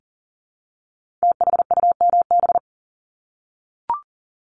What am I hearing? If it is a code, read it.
Morse code, heard as a keyed tone, about 42 wpm, 700 Hz. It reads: THUMB
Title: Multi-letter Phonemes 40wpm